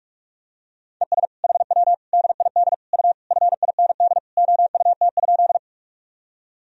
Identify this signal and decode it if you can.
Morse code, heard as a keyed tone, about 46 wpm, 700 Hz. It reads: ES HW DID U FIND OUT?
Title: Sentences from Top 100 Words and Abbreviations 45wpm